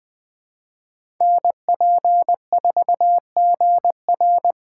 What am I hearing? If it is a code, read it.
Morse code, heard as a keyed tone, about 21 wpm, 700 Hz. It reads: NP4GR
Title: USA Call Signs - 5 Characters - 3x Character Spacing - 20wpm